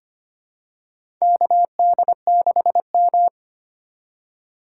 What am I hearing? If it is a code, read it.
Morse code, heard as a keyed tone, about 25 wpm, 700 Hz. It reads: KD6M
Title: USA Call Signs - 4 Characters - 4x Character Spacing - 25wpm